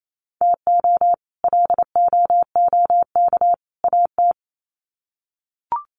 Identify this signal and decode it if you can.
Morse code, heard as a keyed tone, about 28 wpm, 700 Hz. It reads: TO LOOK AT
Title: Sets of 3 Words 28wpm